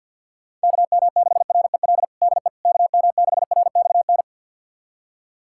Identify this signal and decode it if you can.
Morse code, heard as a keyed tone, about 51 wpm, 700 Hz. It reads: KM6REL DE KM6RXN
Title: USA Callsign DE Callsign - 6 Characters - 50wpm